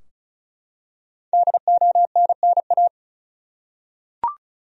Morse code, heard as a keyed tone, about 36 wpm, 700 Hz. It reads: DONNA
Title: USA Common Names - Max 5 Letters - 35wpm